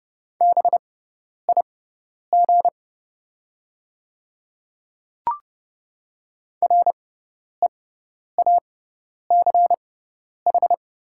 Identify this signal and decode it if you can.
Morse code, heard as a keyed tone, about 30 wpm, spaced as for 10 wpm, 700 Hz. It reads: BIG REACH
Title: 6x Character Spacing - Top 500 Words - 30wpm